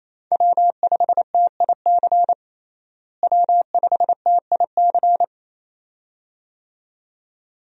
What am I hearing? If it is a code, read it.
Morse code, heard as a keyed tone, about 28 wpm, 700 Hz. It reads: W5TIC W5TIC
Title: USA Call Signs - 5 Characters - Repeated 2x - 28wpm